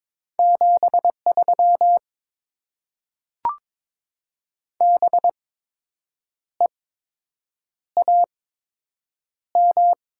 Morse code, heard as a keyed tone, about 22 wpm, spaced as for 7 wpm, 690 Hz. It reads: 73 BEAM
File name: Top 90 QSO Elements - 8x Character Spacing - 22wpm